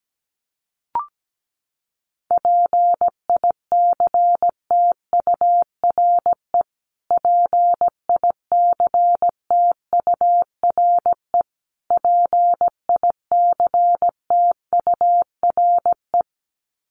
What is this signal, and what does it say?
Morse code, heard as a keyed tone, about 17 wpm, 700 Hz. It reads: PICTURE PICTURE PICTURE
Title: Top 500 Words Repeated 17wpm